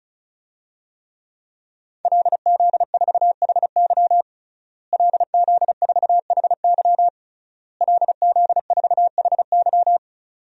Morse code, heard as a keyed tone, about 36 wpm, 700 Hz. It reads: LZ4HY LZ4HY LZ4HY
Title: DXCC-100 Call Signs - 5 Characters - Repeated 3x - 35wpm